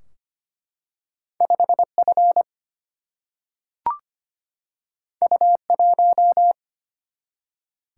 Morse code, heard as a keyed tone, about 25 wpm, 700 Hz. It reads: 5 F U 1
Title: Single-Letter Single-Number 25wpm